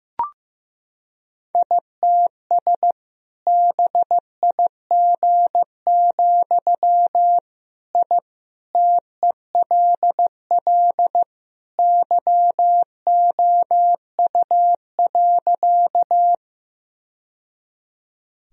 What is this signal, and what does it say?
Morse code, heard as a keyed tone, about 15 wpm, 700 Hz. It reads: ITS BIG, I TELL YOU.
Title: Sentences from Top 1000 Words - Review Part 1 - 15wpm